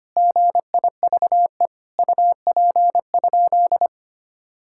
Morse code, heard as a keyed tone, about 25 wpm, 700 Hz. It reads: GIVE UP?
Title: Sentences from Top 1600 Words 25wpm